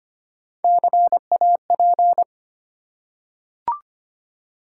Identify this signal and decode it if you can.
Morse code, heard as a keyed tone, about 25 wpm, 700 Hz. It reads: CAP
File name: practice / 25wpm / Easy Words with Hints - 25wpm